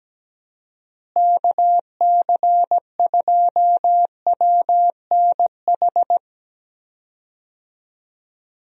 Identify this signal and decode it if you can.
Morse code, heard as a keyed tone, about 17 wpm, 700 Hz. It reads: KC2WNH
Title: USA Call Signs - 6 Characters - Repeated 3x - 17wpm